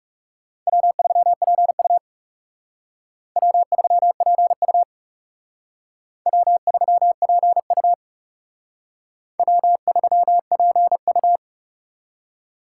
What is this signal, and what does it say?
Morse code, heard as a keyed tone, about 37 wpm, 700 Hz. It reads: W3PU W3PU W3PU W3PU
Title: Speed Racing - USA 1x2 Call Signs 30wpm